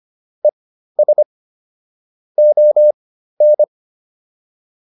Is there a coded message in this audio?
Yes: Morse code from 0.4 to 3.6 s, heard as a keyed tone, about 25 wpm, spaced as for 13 wpm, 600 Hz. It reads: ES ON